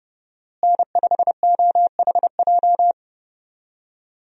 Morse code, heard as a keyed tone, about 30 wpm, 700 Hz. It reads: N5OHJ
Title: USA Call Signs - 5 Characters - Repeated 3x - 30wpm